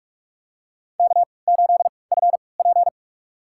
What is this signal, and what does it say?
Morse code, heard as a keyed tone, about 44 wpm, spaced as for 25 wpm, 700 Hz. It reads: K8FP
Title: USA Call Signs - 4 Characters - 3x Character Spacing - 45wpm